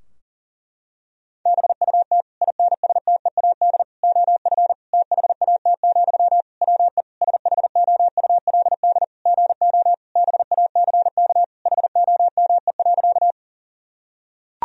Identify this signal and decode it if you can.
Morse code, heard as a keyed tone, about 40 wpm, 700 Hz. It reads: BUT INSTEAD OF THAT, WE SHOULD GO BACK HOME.
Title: Sentences from Top 1000 Words - Review Part 4 - 40wpm